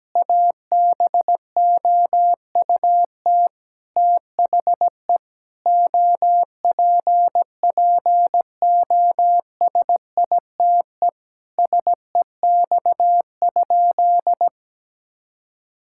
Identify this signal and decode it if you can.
Morse code, heard as a keyed tone, about 17 wpm, 700 Hz. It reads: ABOUT THE OPPOSITE SEX?
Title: Sentences from Top 1600 Words 17wpm